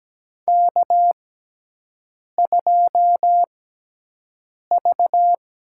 Morse code, heard as a keyed tone, about 17 wpm, spaced as for 6 wpm, 700 Hz. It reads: K2V
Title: USA Call Signs - 4 Characters - 6x Character Spacing - 17wpm